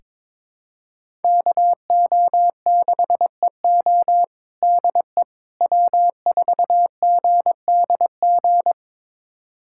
Morse code, heard as a keyed tone, about 22 wpm, 700 Hz. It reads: KO6EO DE W4GDG
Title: USA Callsign DE Callsign - 5 Characters - 22wpm